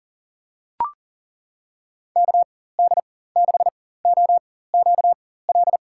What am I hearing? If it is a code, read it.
Morse code, heard as a keyed tone, about 40 wpm, spaced as for 19 wpm, 700 Hz. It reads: KD6OQL
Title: USA Call Signs - 6 Characters - 4x Character Spacing - 40wpm